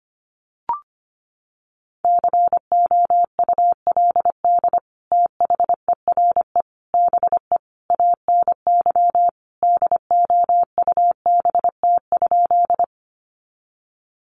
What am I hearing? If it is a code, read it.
Morse code, heard as a keyed tone, about 25 wpm, 700 Hz. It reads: COULD THERE BE ANY DOUBT?
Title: Sentences from Top 1500 Words 25wpm